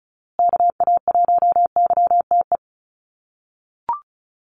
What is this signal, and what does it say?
Morse code, heard as a keyed tone, about 34 wpm, 700 Hz. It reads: KA1YTE
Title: USA Call Signs Repeated 3x 35wpm